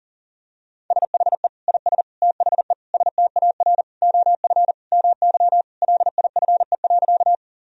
Morse code, heard as a keyed tone, about 40 wpm, 700 Hz. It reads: SHE IS THE STAR OF MY LIFE.
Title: Sentences from Top 1000 Words - Review Part 2 - 40wpm